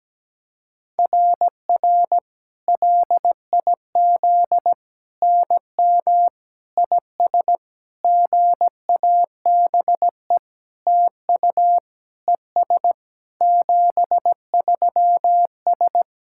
Morse code, heard as a keyed tone, about 17 wpm, 700 Hz. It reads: RR LIZ NM IS GABE TU ES 73S
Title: SOTA QSO Practice 17wpm